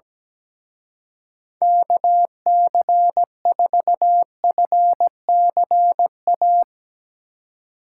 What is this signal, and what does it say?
Morse code, heard as a keyed tone, about 19 wpm, 700 Hz. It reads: KC4FCA
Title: Speed Racing - USA 2x3 Call Signs 17wpm